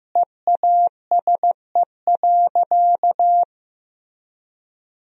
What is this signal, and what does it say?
Morse code, heard as a keyed tone, about 15 wpm, 700 Hz. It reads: EASE.
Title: Sentences from Top 1000 Words - Review Part 1 - 15wpm